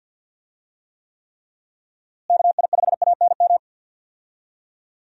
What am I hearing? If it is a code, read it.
Morse code, heard as a keyed tone, about 47 wpm, 700 Hz. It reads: KI5ANM
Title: USA Call Signs - 6 Characters - Repeated 3x - 50wpm